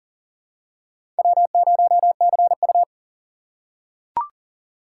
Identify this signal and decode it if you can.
Morse code, heard as a keyed tone, about 40 wpm, 700 Hz. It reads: W0CU
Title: USA Call Signs - 4x Character Spacing - 40wpm